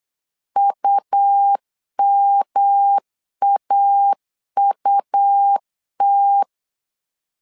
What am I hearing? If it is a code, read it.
Morse code, heard as a keyed tone, about 8 wpm, 800 Hz. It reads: UMAUT
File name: LEZIONI / 3 MWRO / 3 M 8wpm 2300